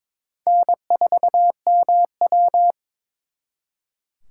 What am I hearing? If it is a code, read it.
Morse code, heard as a keyed tone, about 30 wpm, spaced as for 26 wpm, 700 Hz. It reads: N4MW
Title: Speed Racing - USA 1x2 Call Signs 22wpm